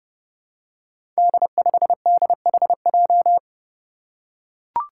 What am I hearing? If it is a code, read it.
Morse code, heard as a keyed tone, about 28 wpm, 700 Hz. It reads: D5DHJ
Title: Five Challenging Letter-Number Combinations 30wpm